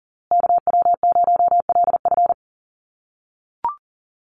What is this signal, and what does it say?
Morse code, heard as a keyed tone, about 40 wpm, 700 Hz. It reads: KW0LF
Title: USA Call Signs Repeated 3x 40wpm